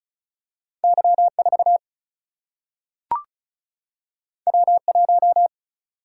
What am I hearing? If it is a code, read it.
Morse code, heard as a keyed tone, about 36 wpm, 700 Hz. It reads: Y 4 W 1
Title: Single-Letter Single-Number 35wpm